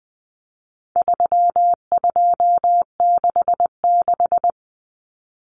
Morse code, heard as a keyed tone, about 20 wpm, 700 Hz. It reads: 3266
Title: Numbers - 4-Digit 20wpm